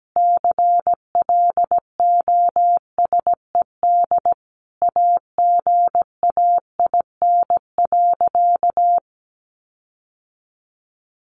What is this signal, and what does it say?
Morse code, heard as a keyed tone, about 17 wpm, 700 Hz. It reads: CLOSED AGAIN.
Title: Sentences from Top 1400 Words 17wpm